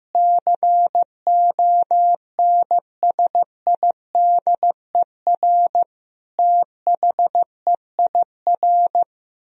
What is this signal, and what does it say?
Morse code, heard as a keyed tone, about 15 wpm, 700 Hz. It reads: CONSIDER THEIR
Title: Sentences from Top 1000 Words - Review Part 4 - 15wpm